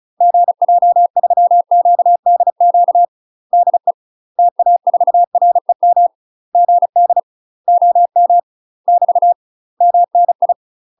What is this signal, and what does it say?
Morse code, heard as a keyed tone, about 34 wpm, 700 Hz. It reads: GJ3QDQ DE TA4REM GD OM <BT> MNI
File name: UK MASTER QSOs TRAINING/35WPM